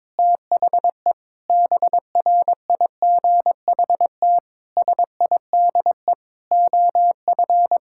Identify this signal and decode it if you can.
Morse code, heard as a keyed tone, about 22 wpm, 700 Hz. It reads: THE BRIGHT SIDE OF
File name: practice / 22wpm / Sets of 8 Words - 22wpm